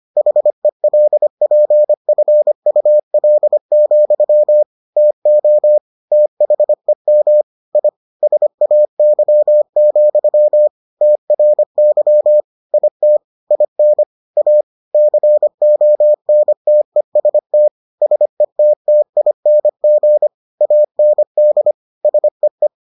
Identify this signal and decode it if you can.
Morse code, heard as a keyed tone, about 25 wpm, 600 Hz. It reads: HELPFUL, TO THEM I SAY, TRY IT IN A CONTEST SETTING AND SEE